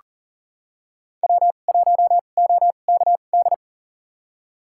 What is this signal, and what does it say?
Morse code, heard as a keyed tone, about 40 wpm, spaced as for 29 wpm, 700 Hz. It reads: W1OKD
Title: USA Call Signs - 5 Characters - 2x Character Spacing - 40wpm